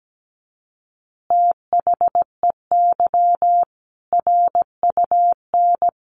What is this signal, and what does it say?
Morse code, heard as a keyed tone, about 17 wpm, 700 Hz. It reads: THEY RUN
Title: Easy Phrases 17wpm